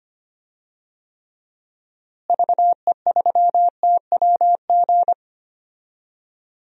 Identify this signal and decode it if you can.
Morse code, heard as a keyed tone, about 25 wpm, 700 Hz. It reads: VE3TWG
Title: DXCC-100 Call Signs - 6 Characters - Repeated 3x - 25wpm